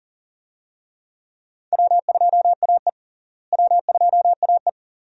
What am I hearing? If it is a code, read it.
Morse code, heard as a keyed tone, about 40 wpm, 700 Hz. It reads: W2AE W2AE
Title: USA Call Signs - 4 Characters - Repeated 2x - 40wpm